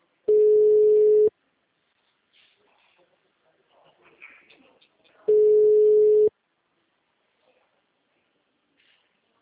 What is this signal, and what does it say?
Ringback tone (Europe, 425 Hz).